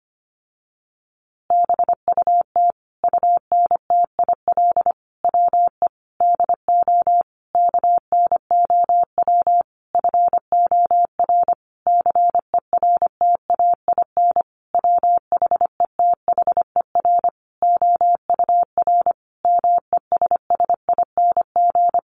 Morse code, heard as a keyed tone, about 25 wpm, 700 Hz. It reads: BUT UNTIL WE DO KNOW FOR CERTAIN WHETHER OUR MESSING
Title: A Sound Of Thunder 25wpm